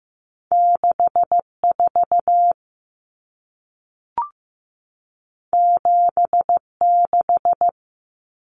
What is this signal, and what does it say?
Morse code, heard as a keyed tone, about 15 wpm, 700 Hz. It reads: 64 76
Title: Numbers - 2-Digit 15wpm